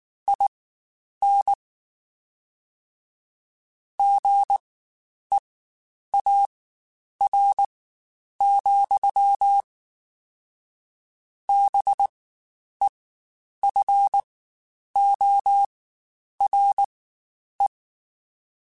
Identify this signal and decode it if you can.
Morse code, heard as a keyed tone, about 19 wpm, spaced as for 9 wpm, 800 Hz. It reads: IN GEAR, BEFORE
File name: Spaced / Chuckles Spaced-mp3s / BrainMouth sp @18